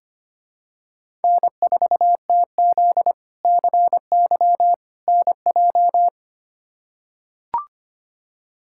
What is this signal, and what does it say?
Morse code, heard as a keyed tone, about 25 wpm, 700 Hz. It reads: N4TZ CY NJ
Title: K1USN SST QSO Practice - 25wpm